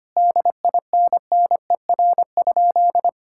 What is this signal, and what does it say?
Morse code, heard as a keyed tone, about 25 wpm, 700 Hz. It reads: DINNER?
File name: practice / 25wpm / Sentences with 4-Word Phrases - 25wpm